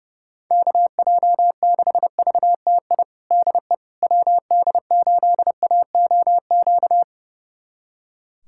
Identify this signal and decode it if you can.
Morse code, heard as a keyed tone, about 30 wpm, 700 Hz. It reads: KJ6VTI DE WD8AOQ
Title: USA Callsign DE Callsign - 6 Characters - 30wpm